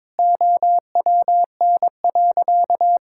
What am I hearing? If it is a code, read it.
Morse code, heard as a keyed tone, about 22 wpm, 700 Hz. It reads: OWN.
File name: Sentences with 7-Word Phrases - 22wpm